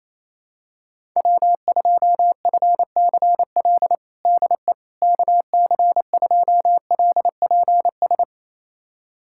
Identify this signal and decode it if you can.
Morse code, heard as a keyed tone, about 28 wpm, 700 Hz. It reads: W2FCL DE KC2LPS
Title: USA Callsign DE Callsign 28wpm